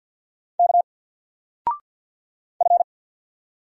Morse code, heard as a keyed tone, about 50 wpm, 700 Hz. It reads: K F
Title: Single Letter-Number 50wpm